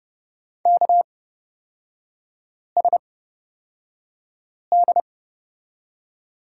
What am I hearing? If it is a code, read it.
Morse code, heard as a keyed tone, about 29 wpm, spaced as for 5 wpm, 700 Hz. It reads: KSD
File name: Lesson 36e - BT - Three Characters 30wpm